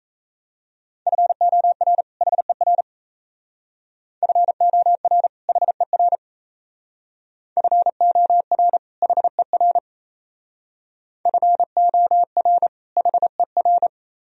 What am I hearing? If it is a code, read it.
Morse code, heard as a keyed tone, about 34 wpm, 700 Hz. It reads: FOR HER FOR HER FOR HER FOR HER
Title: Speed-Racing - Sets of 2 Words 28wpm